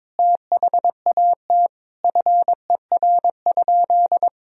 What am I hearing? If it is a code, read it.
Morse code, heard as a keyed tone, about 22 wpm, 700 Hz. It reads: THAT FER?
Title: Sentences from Top 200 Words and Abbreviations 22wpm